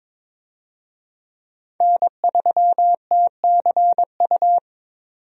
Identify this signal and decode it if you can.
Morse code, heard as a keyed tone, about 22 wpm, 700 Hz. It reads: N3TCU
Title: USA Call Signs - 5 Characters - Repeated 2x - 22wpm